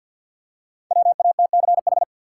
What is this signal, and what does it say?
Morse code, heard as a keyed tone, about 53 wpm, 700 Hz. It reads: WATCH
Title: Five Letter Words - 50wpm